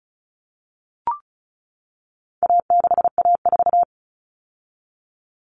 Morse code, heard as a keyed tone, about 34 wpm, 700 Hz. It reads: A6A4
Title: Common Cut Numbers - 4-Digit 35wpm